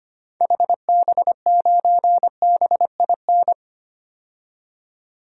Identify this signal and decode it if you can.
Morse code, heard as a keyed tone, about 25 wpm, 700 Hz. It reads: HB9BIN
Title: DXCC-100 Call Signs - 6 Characters - 25wpm